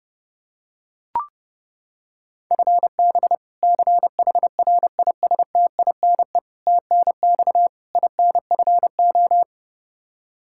Morse code, heard as a keyed tone, about 30 wpm, 700 Hz. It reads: FB CHRISTINE TNX INFO
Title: Practice QSO - Exchange 3 - More Info - 30wpm